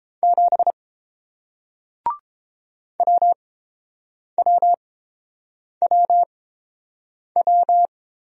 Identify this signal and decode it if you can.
Morse code, heard as a keyed tone, about 29 wpm, spaced as for 7 wpm, 700 Hz. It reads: 7 WWWW